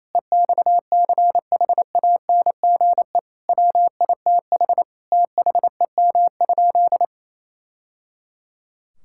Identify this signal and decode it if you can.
Morse code, heard as a keyed tone, about 28 wpm, 700 Hz. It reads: EXCHANGE WITH THEM?
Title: Sentences from Top 1000 Words - Review Part 6 - 28wpm